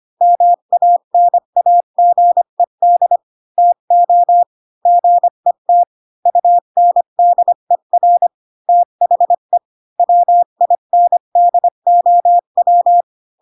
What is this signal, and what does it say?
Morse code, heard as a keyed tone, about 25 wpm, 700 Hz. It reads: MANAGED TO GET UNDER THE WINDOW